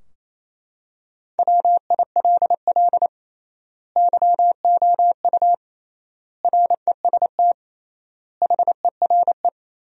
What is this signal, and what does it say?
Morse code, heard as a keyed tone, about 27 wpm, 700 Hz. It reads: WILL YOU REST HERE
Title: 3x Word-Spacing - Sentences from Top 500 Words - 28wpm